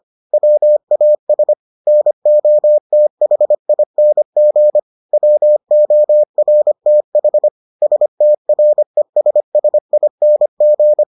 Morse code, heard as a keyed tone, about 25 wpm, 600 Hz. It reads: WAS NOTHING WORTH STRESSING